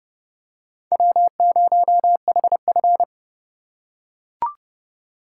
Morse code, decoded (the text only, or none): W0HF